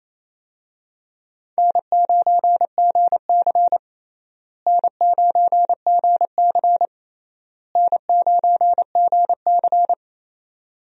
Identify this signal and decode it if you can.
Morse code, heard as a keyed tone, about 28 wpm, 700 Hz. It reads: N9GC N9GC N9GC
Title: USA Call Signs - 4 Characters - Repeated 3x - 28wpm